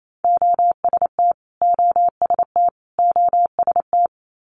Morse code, heard as a keyed tone, about 28 wpm, 700 Hz. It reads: OST OST OST
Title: Multi-letter Phonemes Repeated 3x 28wpm